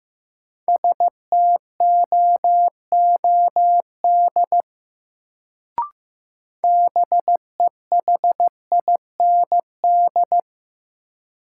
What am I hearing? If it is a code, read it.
Morse code, heard as a keyed tone, about 15 wpm, 700 Hz. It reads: STOOD BEHIND
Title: Sentences from Top 500 Words - New Word Drill - 15wpm